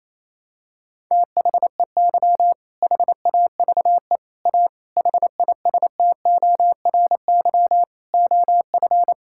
Morse code, heard as a keyed tone, about 28 wpm, 700 Hz. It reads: THEY HAVE A HISTORY OF
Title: Sentences from Top 1000 Words - Review Part 5 - 28wpm